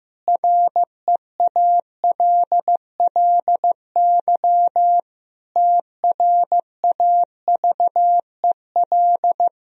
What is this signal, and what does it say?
Morse code, heard as a keyed tone, about 16 wpm, 700 Hz. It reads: REALLY TRAVEL
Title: Sentences from Top 1000 Words - Review Part 2 - 15wpm